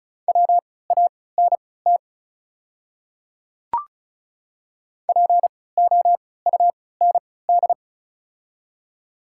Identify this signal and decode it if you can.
Morse code, heard as a keyed tone, about 34 wpm, spaced as for 20 wpm, 700 Hz. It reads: WANT POUND
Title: 3x Character Spacing - Top 500 Words - 35wpm